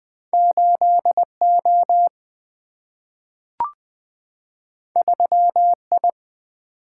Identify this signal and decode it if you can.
Morse code, heard as a keyed tone, about 21 wpm, 700 Hz. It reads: 8 O 3 I
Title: Single-Letter Single-Number 20wpm